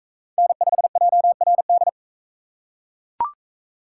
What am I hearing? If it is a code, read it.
Morse code, heard as a keyed tone, about 38 wpm, 700 Hz. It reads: N5JRD